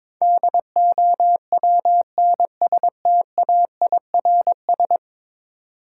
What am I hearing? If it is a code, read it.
Morse code, heard as a keyed tone, about 22 wpm, 700 Hz. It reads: DOWNSTAIRS
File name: practice / 22wpm / 10 Letter Words - First 3 Letters Given - 22wpm